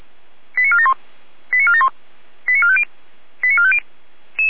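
Five-tone selective call (ZVEI), heard as quick runs of five tones, tones of about 70 ms. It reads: 89471 89471 89450 89450
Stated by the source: ZVEI five-tone call received by SDR, logged on the feuerwehr-orlamuende.org server.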